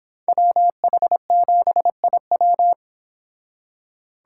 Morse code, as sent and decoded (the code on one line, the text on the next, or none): .-- .... --... .. .--
WH7IW